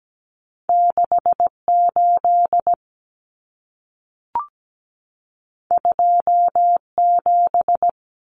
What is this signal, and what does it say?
Morse code, heard as a keyed tone, about 17 wpm, 700 Hz. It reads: 68 27
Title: Numbers - 2-Digit 17wpm